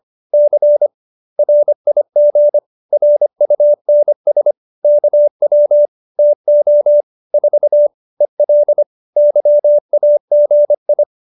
Morse code, heard as a keyed tone, about 25 wpm, 590 Hz. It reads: C RIG RUNS KW TO 4 EL YAGI
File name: qso207b 25